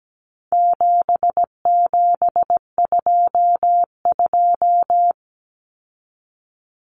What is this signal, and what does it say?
Morse code, heard as a keyed tone, about 17 wpm, 700 Hz. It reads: 7722
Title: Numbers - 4-Digit 17wpm